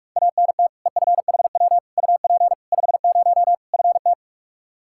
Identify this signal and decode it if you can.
Morse code, heard as a keyed tone, about 42 wpm, 700 Hz. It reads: ANT EFHW UP 50 FT
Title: Practice QSO - Exchange 2 - Info - 45wpm